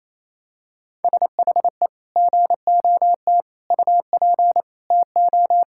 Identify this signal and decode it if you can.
Morse code, heard as a keyed tone, about 28 wpm, 700 Hz. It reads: SHE GOT UP TO
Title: Sentences from Top 1600 Words 28wpm